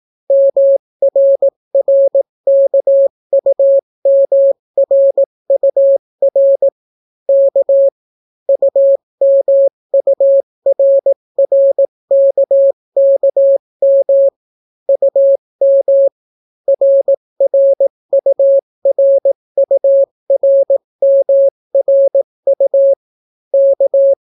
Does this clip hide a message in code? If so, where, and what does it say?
Yes: Morse code from 0.3 to 24.1 s, heard as a keyed tone, about 18 wpm, 550 Hz. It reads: MRRKUMRUR K UMURRKKM UM RRURURMRU K